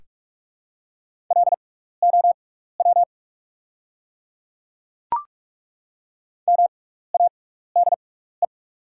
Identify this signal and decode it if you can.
Morse code, heard as a keyed tone, about 42 wpm, spaced as for 15 wpm, 700 Hz. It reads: LOW MADE
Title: 6x Character Spacing - Top 200 Words - 45wpm